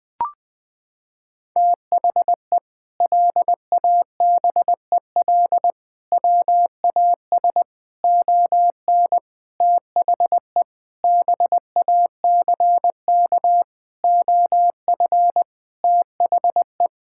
Morse code, heard as a keyed tone, about 20 wpm, 700 Hz. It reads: THE LABEL WAS ON THE BACK OF THE
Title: Sentences with 4-Word Phrases - 20wpm